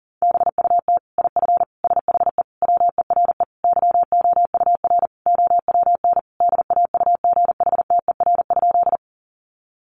Morse code, heard as a keyed tone, about 40 wpm, 700 Hz. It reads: BUT IF SHE WERE YOUR OWN DAUGHTER?
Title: Sentences from Top 1500 Words 40wpm